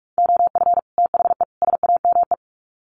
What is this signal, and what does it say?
Morse code, heard as a keyed tone, about 44 wpm, 700 Hz. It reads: OF THE SAME
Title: Sets of 3 Words 45wpm